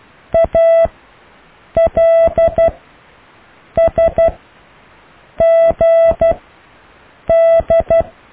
Morse code, heard as a keyed tone, about 12 wpm, spaced as for 6 wpm, 660 Hz. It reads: ALSGD
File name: cw / Alle letters